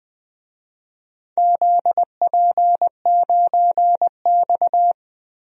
Morse code, heard as a keyed tone, about 20 wpm, 700 Hz. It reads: ZP9X
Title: DXCC-100 Call Signs - 4 Characters - Repeated 2x - 20wpm